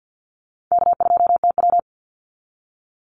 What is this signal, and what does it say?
Morse code, heard as a keyed tone, about 48 wpm, 700 Hz. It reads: K2TW
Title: Easy Call Signs Repeated 3x - 50wpm